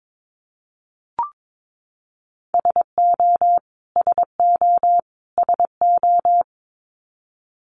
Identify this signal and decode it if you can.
Morse code, heard as a keyed tone, about 22 wpm, 700 Hz. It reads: SO SO SO
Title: Top 500 Words Repeated 22wpm